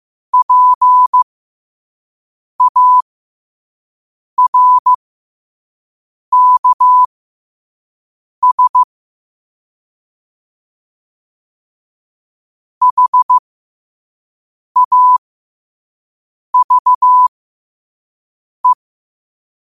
Morse code, heard as a keyed tone, about 15 wpm, spaced as for 5 wpm, 1000 Hz. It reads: PARKS HAVE
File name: qotdfiles / 5wpm